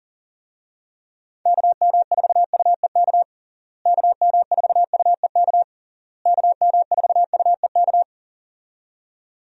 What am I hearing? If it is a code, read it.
Morse code, heard as a keyed tone, about 40 wpm, 700 Hz. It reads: KM4UEK KM4UEK KM4UEK
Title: USA Call Signs - 6 Characters - Repeated 3x - 40wpm